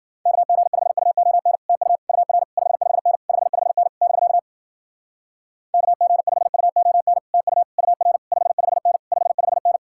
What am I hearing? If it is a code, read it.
Morse code, heard as a keyed tone, about 52 wpm, 700 Hz. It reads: KG5UON TU UR 55N 55N <BK> KG5UON TU UR 55N 55N